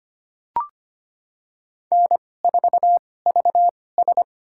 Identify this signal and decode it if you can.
Morse code, heard as a keyed tone, about 25 wpm, spaced as for 18 wpm, 700 Hz. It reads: N4VS
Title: USA Call Signs - 4 Characters - 2x Character Spacing - 25wpm